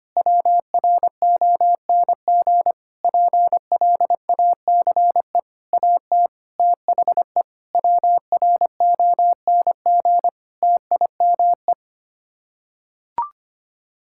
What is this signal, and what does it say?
Morse code, heard as a keyed tone, about 25 wpm, 700 Hz. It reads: WRONG PLACE AT THE WRONG TIME
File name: Sets of 9 Words - 25wpm